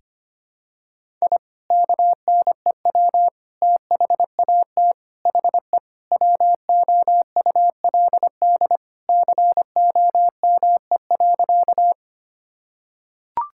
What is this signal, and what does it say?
Morse code, heard as a keyed tone, about 25 wpm, 700 Hz. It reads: I KNEW THAT HE WOULD COME.
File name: practice / 25wpm / Sentences with 2-Word Phrases - 25wpm